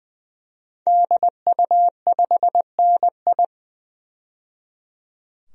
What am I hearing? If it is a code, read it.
Morse code, heard as a keyed tone, about 20 wpm, 700 Hz. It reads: DU5NI
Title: DXCC-100 Call Signs - 5 Characters - 20wpm